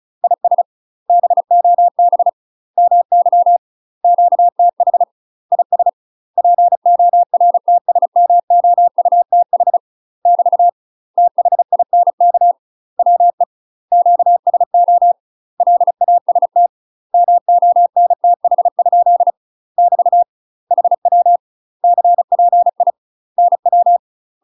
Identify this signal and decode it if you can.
Morse code, heard as a keyed tone, about 36 wpm, 700 Hz. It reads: IS BOB MY QTH IS PORTSMOUTH <BT> THINK WE QSO LAST MONTH? <BT> HW CPI NW
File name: UK MASTER QSOs TRAINING / 35WPM / UK MASTER QSOs 0002